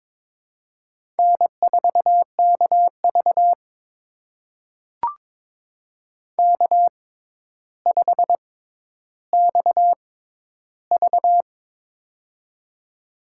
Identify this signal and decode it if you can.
Morse code, heard as a keyed tone, about 22 wpm, spaced as for 11 wpm, 700 Hz. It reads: N4KV K5XV
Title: USA Call Signs - 4 Characters - 6x Character Spacing - 22wpm